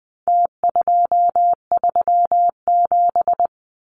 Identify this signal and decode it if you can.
Morse code, heard as a keyed tone, about 20 wpm, 700 Hz. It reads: T237
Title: Numbers - Contest Style - 4-Digit 20wpm